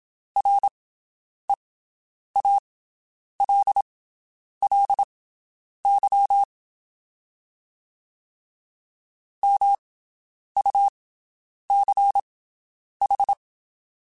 Morse code, heard as a keyed tone, about 27 wpm, spaced as for 9 wpm, 800 Hz. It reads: REALLY MUCH
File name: (spaced) / QSOs Spaced-mp3s / WD0BGP 1 @25 sp